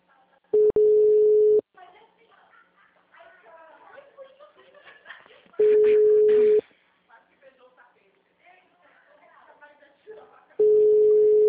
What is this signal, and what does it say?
Ringback tone (Europe, 425 Hz), with speech.